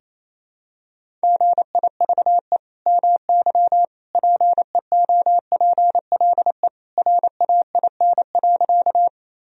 Morse code, heard as a keyed tone, about 28 wpm, 700 Hz. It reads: GIVE MY PEOPLE RAIN.
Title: Sentences from Top 1000 Words - Review Part 1 - 28wpm